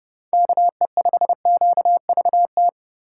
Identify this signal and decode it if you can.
Morse code, heard as a keyed tone, about 30 wpm, 700 Hz. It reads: KE5QVT
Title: USA Call Signs - 6 Characters - 3x Character Spacing - 30wpm